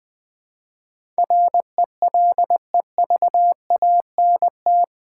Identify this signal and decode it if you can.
Morse code, heard as a keyed tone, about 20 wpm, 700 Hz. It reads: RELEVANT
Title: Build A Word - 3 Letters At A Time - 8-Letter Words - 20wpm